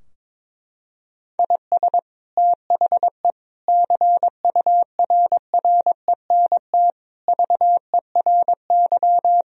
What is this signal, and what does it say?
Morse code, heard as a keyed tone, about 21 wpm, 700 Hz. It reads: IS THE CURRENT VERY
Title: Sentences from Top 1000 Words - Review Part 4 - 22wpm